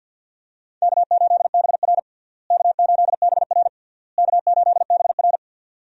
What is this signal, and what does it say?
Morse code, heard as a keyed tone, about 47 wpm, 700 Hz. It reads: K8BR K8BR K8BR
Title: USA Call Signs - 4 Characters - Repeated 3x - 50wpm